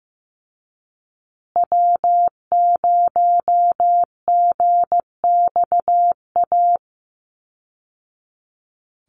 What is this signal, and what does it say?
Morse code, heard as a keyed tone, about 15 wpm, 700 Hz. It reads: W0GXA
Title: Call Signs Repeated 2x - 15wpm